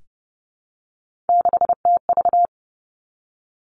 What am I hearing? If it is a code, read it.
Morse code, heard as a keyed tone, about 32 wpm, 700 Hz. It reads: 6TV
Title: Three Challenging Letter-Number Combinations 30wpm